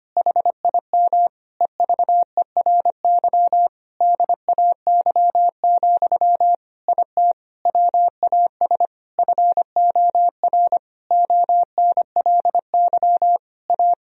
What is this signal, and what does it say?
Morse code, heard as a keyed tone, about 25 wpm, 700 Hz. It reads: HIM EVERY DAY, IT WAS FOR ONLY A
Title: Sentences from Top 1000 Words - Review Part 3 - 25wpm